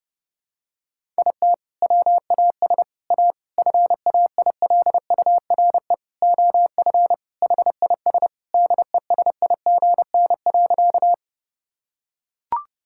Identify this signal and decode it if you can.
Morse code, heard as a keyed tone, about 30 wpm, 700 Hz. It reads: IT WAS A FAILURE OF HIS DESIGN.
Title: Sentences from Top 1000 Words - Review Part 6 - 30wpm